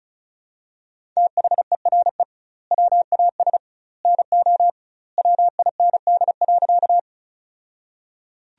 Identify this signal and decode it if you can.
Morse code, heard as a keyed tone, about 36 wpm, 700 Hz. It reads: THERE WAS NO WIND.
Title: 2x Word-Spacing - Sentences from Top 400 Words - 35wpm